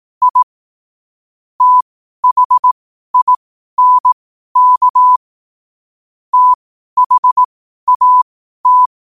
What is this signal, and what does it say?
Morse code, heard as a keyed tone, about 18 wpm, spaced as for 12 wpm, 1000 Hz. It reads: I THINK THAT
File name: qotdfiles / 12wpm